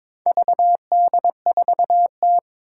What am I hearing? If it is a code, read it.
Morse code, heard as a keyed tone, about 22 wpm, 700 Hz. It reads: VD4T
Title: Four Challenging Letter-Number Combinations 22wpm